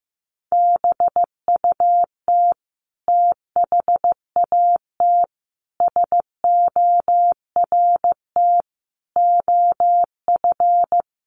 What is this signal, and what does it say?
Morse code, heard as a keyed tone, about 14 wpm, 700 Hz. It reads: BUT THAT SORT OF
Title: Sentences from Top 1300 Words 15wpm